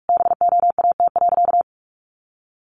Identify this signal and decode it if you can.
Morse code, heard as a keyed tone, about 44 wpm, 700 Hz. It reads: BOAT.
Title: Sentences from Top 1400 Words 45wpm